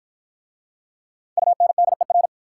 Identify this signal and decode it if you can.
Morse code, heard as a keyed tone, about 48 wpm, 700 Hz. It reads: UNDER